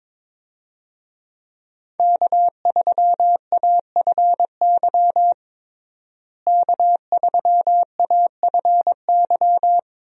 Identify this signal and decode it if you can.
Morse code, heard as a keyed tone, about 22 wpm, 700 Hz. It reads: K3AFY K3AFY
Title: USA Call Signs - 5 Characters - Repeated 2x - 22wpm